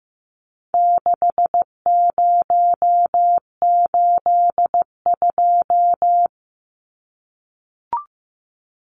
Morse code, heard as a keyed tone, about 15 wpm, 700 Hz. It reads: 6082
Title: Numbers - 4-Digit 15wpm